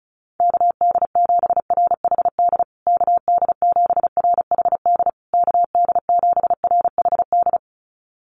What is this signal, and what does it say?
Morse code, heard as a keyed tone, about 34 wpm, 700 Hz. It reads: KD7RHD KD7RHD KD7RHD
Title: USA Call Signs Repeated 3x 35wpm